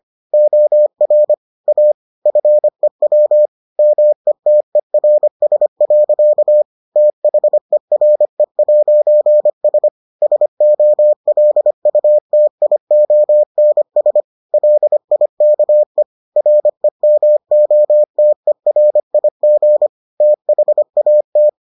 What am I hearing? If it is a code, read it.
Morse code, heard as a keyed tone, about 25 wpm, 600 Hz. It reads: OR A FEW METERS. THERE'S SOLUTIONS LIKE REMOTERIG THAT